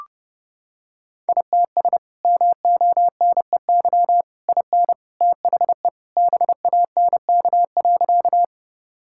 Morse code, heard as a keyed tone, about 32 wpm, 700 Hz. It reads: ITS MONEY IN THE BANK.
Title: Sentences from Top 1000 Words - Review Part 6 - 30wpm